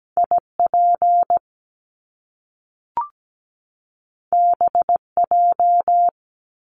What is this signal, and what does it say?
Morse code, heard as a keyed tone, about 17 wpm, 700 Hz. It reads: IP BJ
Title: Top Two Letter Combinations in Words 17wpm